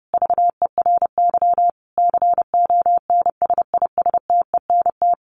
Morse code, heard as a keyed tone, about 30 wpm, 700 Hz. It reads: VERY CONSISTENT
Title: Sentences from Top 1500 Words 30wpm